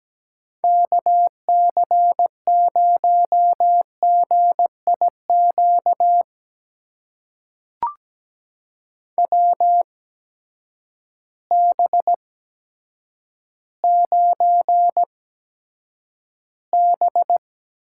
Morse code, heard as a keyed tone, about 17 wpm, spaced as for 8 wpm, 700 Hz. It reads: KC0GIQ WB9B
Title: USA Call Signs - 6 Characters - 8x Character Spacing - 17wpm